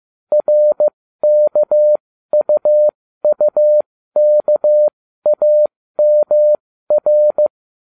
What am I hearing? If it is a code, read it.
Morse code, heard as a keyed tone, about 15 wpm, spaced as for 13 wpm, 600 Hz. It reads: RKUUKAMR